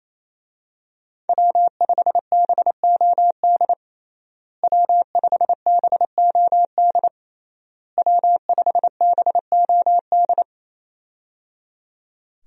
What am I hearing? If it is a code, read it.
Morse code, heard as a keyed tone, about 28 wpm, 700 Hz. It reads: W5BOD W5BOD W5BOD
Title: USA Call Signs - 5 Characters - Repeated 3x - 28wpm